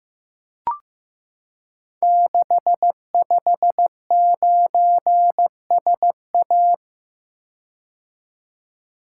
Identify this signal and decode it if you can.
Morse code, heard as a keyed tone, about 15 wpm, 700 Hz. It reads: 659SA
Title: Code Groups - 5 Letter:Numbers - 15wpm